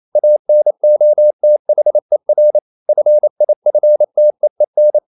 Morse code, heard as a keyed tone, about 28 wpm, 600 Hz. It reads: ANOTHER FIFTEEN
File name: Never 0027wpm